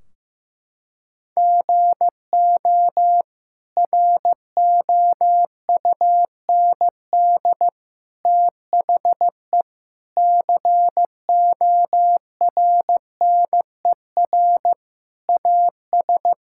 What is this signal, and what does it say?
Morse code, heard as a keyed tone, about 15 wpm, 700 Hz. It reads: GO ROUND THE CORNER AS
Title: Sentences from Top 1600 Words 15wpm